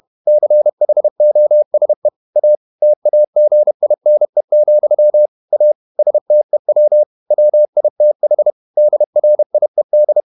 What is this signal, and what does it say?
Morse code, heard as a keyed tone, about 31 wpm, 600 Hz. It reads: CHOSE A TAGINE, A STEW WITH DRIED
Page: Never 31wpm | CW med Gnister